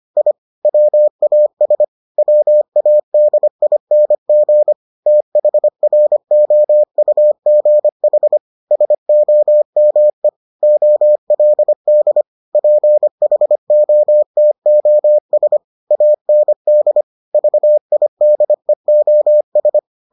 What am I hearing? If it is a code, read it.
Morse code, heard as a keyed tone, about 25 wpm, 600 Hz. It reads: I WAS WADING THROUGH SOME OLD PHOTOS AND VIDEOS